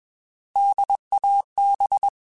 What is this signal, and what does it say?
Morse code, heard as a keyed tone, about 21 wpm, 800 Hz. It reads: DAB
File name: Alphabetical / Alphabetical mp3s / Ds @20